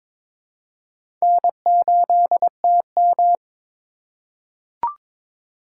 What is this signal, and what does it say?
Morse code, heard as a keyed tone, about 21 wpm, 700 Hz. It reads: N8TM
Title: USA Call Signs - 4 Characters - 3x Character Spacing - 22wpm